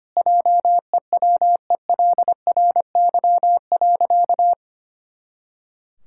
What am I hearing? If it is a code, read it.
Morse code, heard as a keyed tone, about 25 wpm, 700 Hz. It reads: JEWELRY.
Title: Pangram Challenge v2 - 25wpm